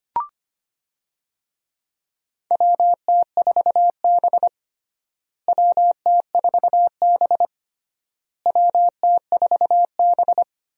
Morse code, heard as a keyed tone, about 25 wpm, 700 Hz. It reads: WT4B WT4B WT4B
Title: USA Call Signs - 4 Characters - Repeated 3x - 25wpm